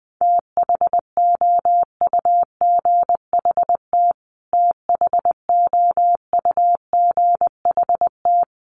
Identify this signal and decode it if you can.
Morse code, heard as a keyed tone, about 19 wpm, 700 Hz. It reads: THOUGHT THOUGHT
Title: Top 500 Words Repeated 20wpm